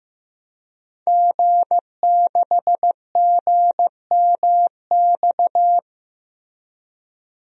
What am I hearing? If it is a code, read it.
Morse code, heard as a keyed tone, about 15 wpm, 700 Hz. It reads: G6GMX
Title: DXCC-100 Call Signs - 5 Characters - Repeated 3x - 15wpm